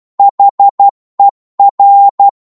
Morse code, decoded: HER